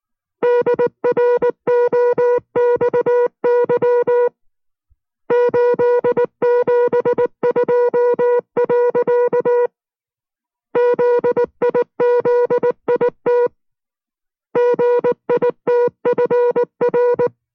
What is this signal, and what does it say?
Morse code, heard as a keyed tone, about 19 wpm, 480 Hz. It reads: DROXY 872. ZIZIT GITFR